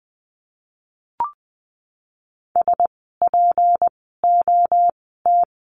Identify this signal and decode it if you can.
Morse code, heard as a keyed tone, about 20 wpm, spaced as for 15 wpm, 700 Hz. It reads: SPOT
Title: 153 Lesson 25f - Letter p - Words 20wpm